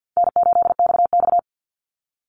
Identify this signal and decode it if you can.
Morse code, heard as a keyed tone, about 50 wpm, 700 Hz. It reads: N8XX
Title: Easy Call Signs - 50wpm